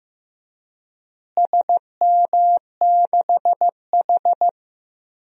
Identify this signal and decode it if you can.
Morse code, heard as a keyed tone, about 15 wpm, 700 Hz. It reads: SM6H
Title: DXCC-100 Call Signs - 4 Characters - Repeated 2x - 15wpm